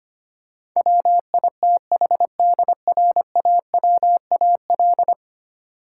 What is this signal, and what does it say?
Morse code, heard as a keyed tone, about 25 wpm, 700 Hz. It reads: WITHDRAWAL
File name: Ten Letter Words - 25wpm